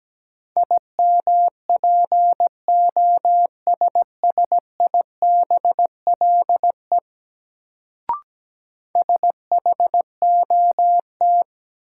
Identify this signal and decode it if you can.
Morse code, heard as a keyed tone, about 17 wpm, 700 Hz. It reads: IMPOSSIBLE SHOT
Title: Sentences from Top 1100 Words - New Word Drill - 17wpm